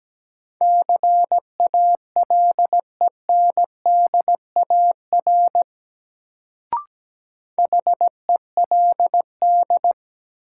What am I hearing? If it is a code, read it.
Morse code, heard as a keyed tone, about 17 wpm, 700 Hz. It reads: CALENDAR HELD
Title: Sentences from Top 1600 Words - New Word Drill - 17wpm